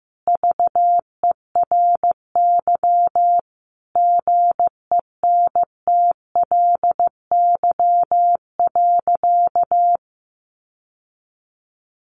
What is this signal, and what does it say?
Morse code, heard as a keyed tone, about 15 wpm, 700 Hz. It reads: VERY GENTLY.
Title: Sentences from Top 1500 Words 15wpm